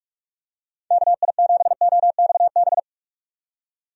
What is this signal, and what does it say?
Morse code, heard as a keyed tone, about 44 wpm, 700 Hz. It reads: KI7OXB
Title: USA Call Signs - 6 Characters - 45wpm